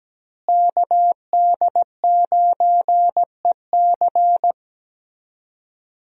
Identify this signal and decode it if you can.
Morse code, heard as a keyed tone, about 17 wpm, 700 Hz. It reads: KD9EC
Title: USA Call Signs - 5 Characters - Repeated 2x - 17wpm